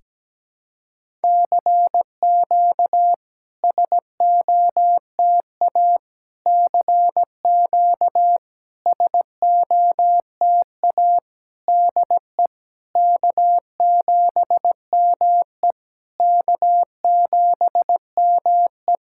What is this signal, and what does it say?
Morse code, heard as a keyed tone, about 17 wpm, 700 Hz. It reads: CQ SOTA CQ SOTA DE K7ME K7ME
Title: SOTA QSO Practice 17wpm